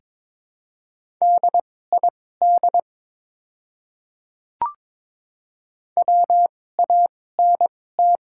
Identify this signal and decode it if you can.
Morse code, heard as a keyed tone, about 22 wpm, spaced as for 16 wpm, 700 Hz. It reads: DID WANT
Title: 2x Character Spacing - Top 200 Words - 22wpm